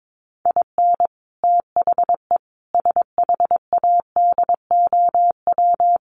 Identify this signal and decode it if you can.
Morse code, heard as a keyed tone, about 22 wpm, 700 Hz. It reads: IN THE SHADOW
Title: Sets of 5 Words Encore 22wpm